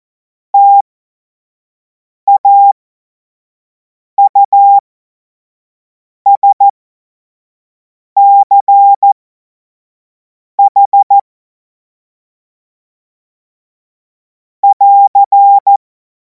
Morse code, heard as a keyed tone, about 14 wpm, spaced as for 5 wpm, 800 Hz. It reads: TAUSCH <AR>
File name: Morsekurs / Witze / Witz 23. Juni 2026 25 F70